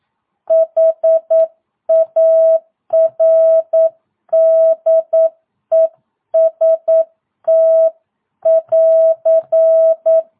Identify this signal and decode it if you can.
Morse code, heard as a keyed tone, about 9 wpm, 650 Hz. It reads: HARDEST <AR>
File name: jh8cbh / 2022cw